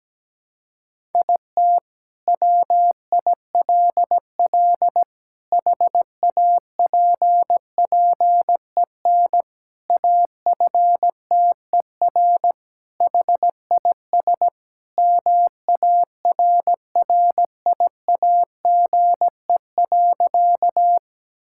Morse code, heard as a keyed tone, about 17 wpm, 700 Hz. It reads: IT WILL HAPPEN AFTER HIS MARRIAGE.
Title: Sentences from Top 1000 Words - Review Part 5 - 17wpm